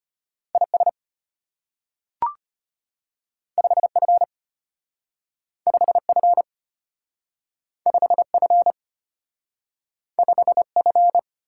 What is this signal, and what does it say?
Morse code, heard as a keyed tone, about 33 wpm, 700 Hz. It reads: IS 5F 5F 5F 5F